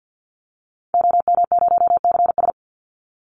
Morse code, heard as a keyed tone, about 50 wpm, 700 Hz. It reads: OM0CS
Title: Call Signs Repeated 3x - 50wpm